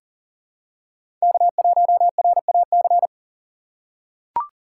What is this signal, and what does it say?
Morse code, heard as a keyed tone, about 39 wpm, 700 Hz. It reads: K1RAC
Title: USA Call Signs - 5 Characters - 2x Character Spacing - 40wpm